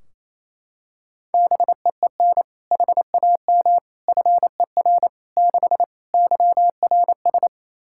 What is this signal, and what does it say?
Morse code, heard as a keyed tone, about 27 wpm, 700 Hz. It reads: BEEN HAM FER 6 YRS
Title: K1USN SST QSO Practice 28wpm